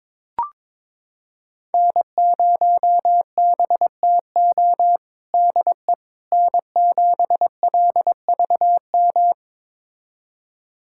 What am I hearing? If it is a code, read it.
Morse code, heard as a keyed tone, about 22 wpm, 700 Hz. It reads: N0BTO DE N7LVM
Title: USA Callsign DE Callsign - 5 Characters - 22wpm